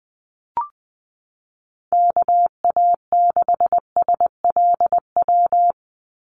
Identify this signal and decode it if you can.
Morse code, heard as a keyed tone, about 20 wpm, 700 Hz. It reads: KA6SLW
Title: USA Call Signs 20wpm